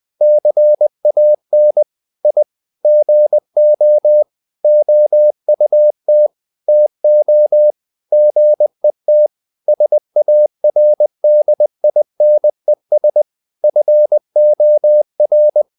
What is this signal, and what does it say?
Morse code, heard as a keyed tone, about 20 wpm, 600 Hz. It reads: CAN I GO OUT TO GET SARDINES FOR
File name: the old man and the sea / the old man and the sea part 1 20wpm